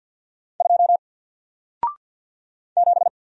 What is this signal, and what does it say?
Morse code, heard as a keyed tone, about 49 wpm, 700 Hz. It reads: 2 7
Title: Single Character (Letter:Number:Punctuation:Prosign) Alt 50wpm